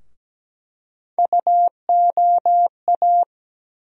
Morse code, heard as a keyed tone, about 17 wpm, 700 Hz. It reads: UOA
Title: USA Call Sign Postfixes - 3 Letters 17wpm